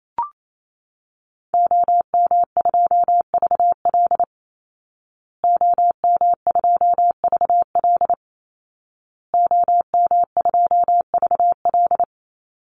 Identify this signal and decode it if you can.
Morse code, heard as a keyed tone, about 28 wpm, 700 Hz. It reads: OM2VL OM2VL OM2VL
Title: Call Signs Repeated 2x - 28wpm